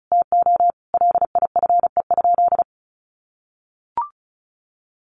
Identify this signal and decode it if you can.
Morse code, heard as a keyed tone, about 34 wpm, 700 Hz. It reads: TO LIFE?
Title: Sentences from Top 1500 Words 35wpm